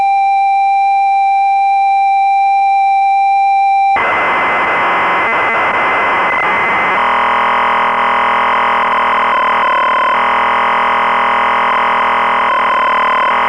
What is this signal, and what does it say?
Computer data on cassette.